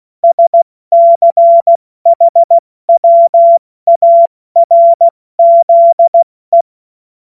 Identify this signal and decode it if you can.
Morse code, heard as a keyed tone, about 16 wpm, 680 Hz. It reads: SCHWARZE
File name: Morsekurs / Witze / Witz 20 Februar 2024 70 F80